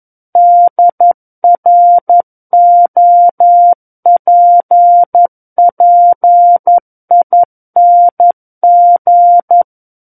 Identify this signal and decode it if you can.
Morse code, heard as a keyed tone, about 11 wpm, 700 Hz. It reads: DROPPING
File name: 11 WPM